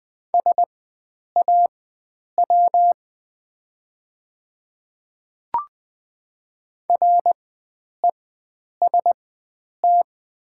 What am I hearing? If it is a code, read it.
Morse code, heard as a keyed tone, about 20 wpm, spaced as for 9 wpm, 700 Hz. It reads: SAW REST
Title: 4x Character Spacing - Top 500 Words - 20wpm